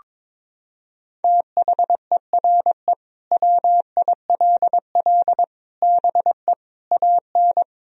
Morse code, heard as a keyed tone, about 22 wpm, 700 Hz. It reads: THERE WILL BE AN
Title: Sentences from Top 1000 Words - Review Part 4 - 22wpm